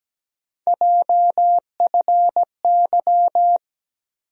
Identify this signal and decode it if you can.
Morse code, heard as a keyed tone, about 17 wpm, 700 Hz. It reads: JFY
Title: USA Call Sign Postfixes - 3 Letters 17wpm